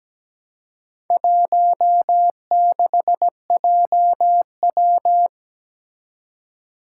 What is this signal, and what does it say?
Morse code, heard as a keyed tone, about 17 wpm, 700 Hz. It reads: 16JW
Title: Four Challenging Letter-Number Combinations 17wpm